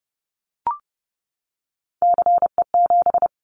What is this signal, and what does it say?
Morse code, heard as a keyed tone, about 30 wpm, 700 Hz. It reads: CE7
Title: Callsign Prefixes for Top 100 DXCC - 30wpm